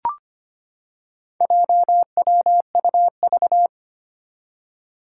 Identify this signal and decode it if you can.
Morse code, heard as a keyed tone, about 25 wpm, 700 Hz. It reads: JWUV
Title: Four Challenging Letter-Number Combinations 25wpm